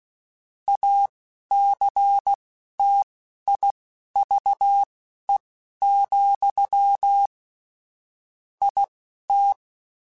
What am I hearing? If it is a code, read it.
Morse code, heard as a keyed tone, about 16 wpm, spaced as for 11 wpm, 800 Hz. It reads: ACTIVE, IT
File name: Technical Subjects / Technical Subjects-mp3s / AND Gate @10